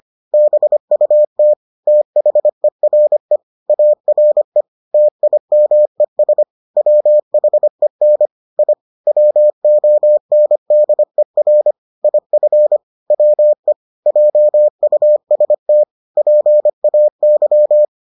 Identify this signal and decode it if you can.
Morse code, heard as a keyed tone, about 25 wpm, 600 Hz. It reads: BUT THERE ARE TIMES WHEN I WONDER IF WE JUST PAY